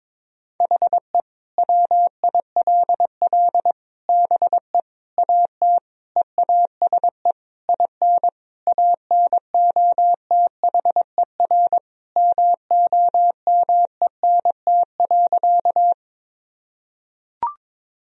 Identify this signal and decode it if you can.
Morse code, heard as a keyed tone, about 22 wpm, 700 Hz. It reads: HE WILL BE AT EASE IN ANOTHER MOMENT.
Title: Sentences from Top 1000 Words - Review Part 1 - 22wpm